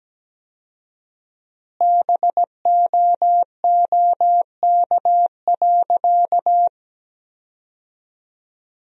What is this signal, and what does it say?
Morse code, heard as a keyed tone, about 17 wpm, 700 Hz. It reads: BOOK.
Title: 4x Word-Spacing - Sentences from Top 300 Words - 17wpm